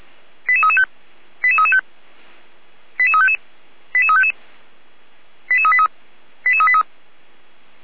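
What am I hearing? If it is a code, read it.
Five-tone selective call (ZVEI), heard as quick runs of five tones, tones of about 80 ms. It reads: 88335 88335 88355 88355 88383 88383
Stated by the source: ZVEI five-tone selective call received on an SDR.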